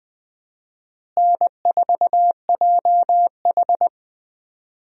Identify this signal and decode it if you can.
Morse code, heard as a keyed tone, about 20 wpm, 700 Hz. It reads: N4JH
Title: USA Call Signs - 4 Characters - Repeated 3x - 20wpm